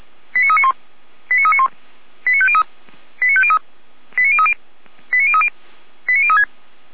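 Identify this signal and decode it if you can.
Five-tone selective call (ZVEI), heard as quick runs of five tones, tones of about 70 ms. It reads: 89391 89391 89503 89503 89030 89030 89036
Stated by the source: ZVEI five-tone selective call 89187 received on an SDR.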